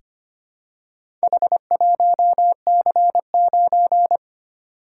Morse code, heard as a keyed tone, about 25 wpm, 700 Hz. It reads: H1C9
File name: practice / 25wpm / Four Letter-Number Combinations - 25wpm